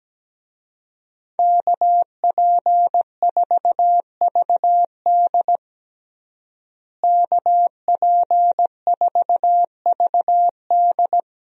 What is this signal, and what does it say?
Morse code, heard as a keyed tone, about 17 wpm, 700 Hz. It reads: KP4VD KP4VD
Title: DXCC-100 Call Signs - 5 Characters - Repeated 3x - 17wpm